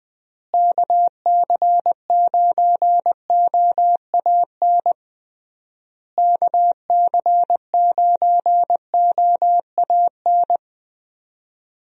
Morse code, heard as a keyed tone, about 20 wpm, 700 Hz. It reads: KC9OAN KC9OAN
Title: USA Call Signs - 6 Characters - Repeated 3x - 20wpm